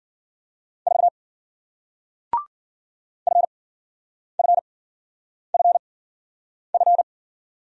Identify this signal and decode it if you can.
Morse code, heard as a keyed tone, about 54 wpm, spaced as for 9 wpm, 700 Hz. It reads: 4 FFFF